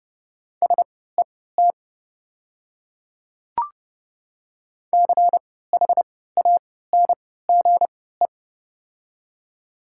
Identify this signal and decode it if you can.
Morse code, heard as a keyed tone, about 31 wpm, spaced as for 17 wpm, 700 Hz. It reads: SET CHANGE
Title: 3x Character Spacing - Top 200 Words - 30wpm